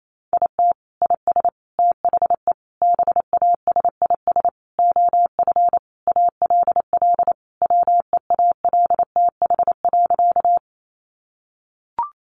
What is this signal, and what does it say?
Morse code, heard as a keyed tone, about 28 wpm, 700 Hz. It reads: IT IS THE BASIS OF ALL WEALTH.
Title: Sentences from Top 1200 Words 28wpm